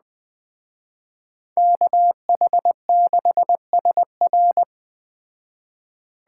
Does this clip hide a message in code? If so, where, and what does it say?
Yes: Morse code from 1.6 to 4.6 s, heard as a keyed tone, about 20 wpm, 700 Hz. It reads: KH6SR